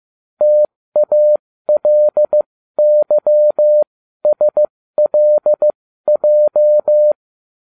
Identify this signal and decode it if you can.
Morse code, heard as a keyed tone, about 15 wpm, spaced as for 13 wpm, 600 Hz. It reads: TALYSLJ